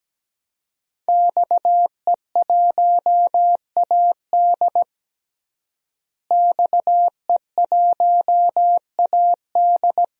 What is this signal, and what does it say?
Morse code, heard as a keyed tone, about 17 wpm, 700 Hz. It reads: XE1AD XE1AD
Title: DXCC-100 Call Signs - 5 Characters - Repeated 2x - 17wpm